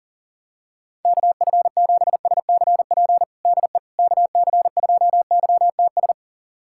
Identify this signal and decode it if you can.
Morse code, heard as a keyed tone, about 40 wpm, 700 Hz. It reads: KF7SCP DE KC2YTS
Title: USA Callsign DE Callsign - 6 Characters - 40wpm